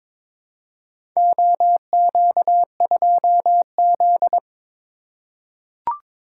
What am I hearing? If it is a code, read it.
Morse code, heard as a keyed tone, about 22 wpm, 700 Hz. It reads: OQ2Z
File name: practice / 22wpm / Four Letter-Number Combinations - 22wpm